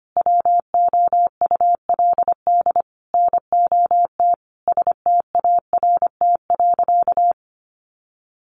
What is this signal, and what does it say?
Morse code, heard as a keyed tone, about 25 wpm, 700 Hz. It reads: WOULD NOT START.
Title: Sentences from Top 1500 Words 25wpm